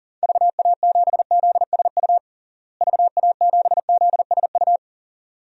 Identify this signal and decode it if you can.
Morse code, heard as a keyed tone, about 40 wpm, 700 Hz. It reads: VA7ZSU VA7ZSU
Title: DXCC-100 Call Signs - 6 Characters - Repeated 2x - 40wpm